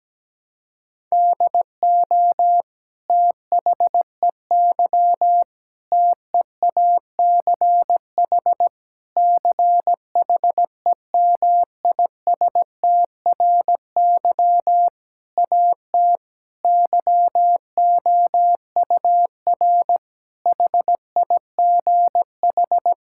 Morse code, heard as a keyed tone, about 17 wpm, 700 Hz. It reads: DO THEY TEACH CHEMISTRY AT YOUR HIGH
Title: Sentences from Top 1000 Words - Review Part 6 - 17wpm